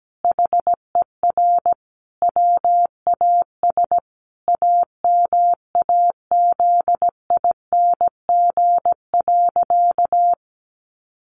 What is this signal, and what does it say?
Morse code, heard as a keyed tone, about 17 wpm, 700 Hz. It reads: HER WAS AMAZING.
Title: Sentences from Top 1500 Words 17wpm